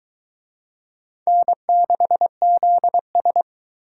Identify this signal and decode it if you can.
Morse code, heard as a keyed tone, about 24 wpm, 700 Hz. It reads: N6ZS